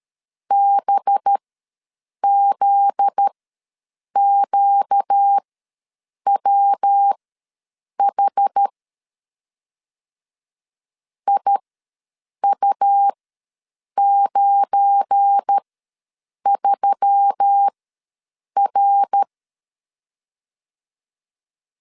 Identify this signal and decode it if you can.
Morse code, heard as a keyed tone, about 13 wpm, spaced as for 7 wpm, 800 Hz. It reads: BZQWH IU93R
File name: 7 GF9Z ws 12wpm